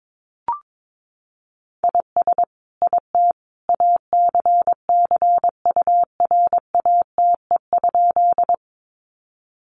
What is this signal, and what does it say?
Morse code, heard as a keyed tone, about 22 wpm, 700 Hz. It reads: IS IT ACCURATE?
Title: Sentences from Top 1200 Words 22wpm